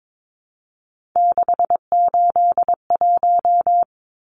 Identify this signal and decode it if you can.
Morse code, heard as a keyed tone, about 22 wpm, 700 Hz. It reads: 681
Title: Numbers - 3-Digit 22wpm